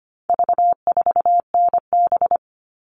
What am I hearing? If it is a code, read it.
Morse code, heard as a keyed tone, about 25 wpm, 700 Hz. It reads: V4NB
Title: All Cut Numbers - 4-Digit 25wpm